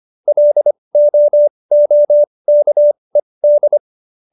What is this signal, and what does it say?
Morse code, heard as a keyed tone, about 25 wpm, spaced as for 20 wpm, 590 Hz. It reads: LOOKED